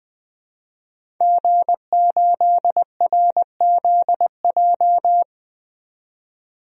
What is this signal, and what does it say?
Morse code, heard as a keyed tone, about 20 wpm, 700 Hz. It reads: G8RZJ
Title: DXCC-100 Call Signs - 5 Characters - 20wpm